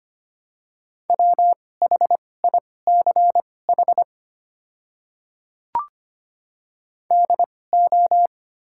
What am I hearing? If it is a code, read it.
Morse code, heard as a keyed tone, about 25 wpm, spaced as for 18 wpm, 700 Hz. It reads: WHICH DO
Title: 2x Character Spacing - Top 100 Words - 25wpm